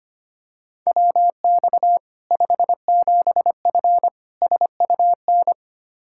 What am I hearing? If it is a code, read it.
Morse code, heard as a keyed tone, about 25 wpm, 700 Hz. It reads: WX 57F SUN
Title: K1USN SST QSO Practice 25wpm